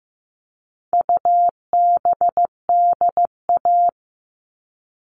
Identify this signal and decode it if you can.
Morse code, heard as a keyed tone, about 15 wpm, 700 Hz. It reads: UBDA
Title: All Cut Numbers - 4-Digit 15wpm